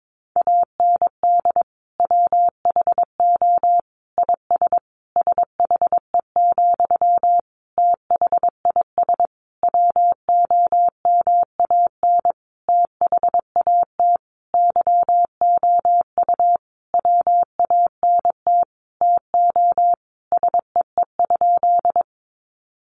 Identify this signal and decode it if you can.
Morse code, heard as a keyed tone, about 22 wpm, 700 Hz. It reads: AND WHO IS SHE, THIS WOMAN THAT YOU WANT TO SEE?
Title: Sentences from Top 1100 Words 22wpm